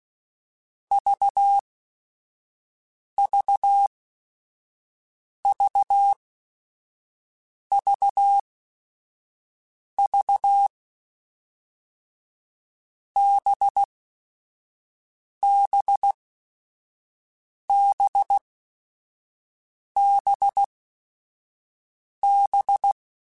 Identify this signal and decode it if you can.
Morse code, heard as a keyed tone, about 16 wpm, spaced as for 5 wpm, 800 Hz. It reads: VVVVV BBBBB
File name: Groups@5 / Groups@5 mp3s / VB46@5